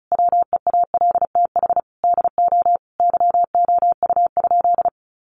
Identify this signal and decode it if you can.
Morse code, heard as a keyed tone, about 34 wpm, 700 Hz. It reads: WEALTH DO YOU?
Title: Sentences from Top 1200 Words 35wpm